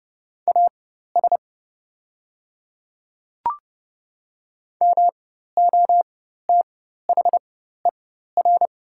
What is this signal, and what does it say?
Morse code, heard as a keyed tone, about 30 wpm, spaced as for 14 wpm, 690 Hz. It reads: AS MOTHER
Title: 4x Character Spacing - Top 200 Words - 30wpm